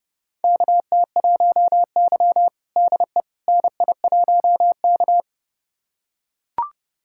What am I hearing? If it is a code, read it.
Morse code, heard as a keyed tone, about 30 wpm, 700 Hz. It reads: KT1Y DE NI1K
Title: USA Callsign DE Callsign - 4 Characters - 30wpm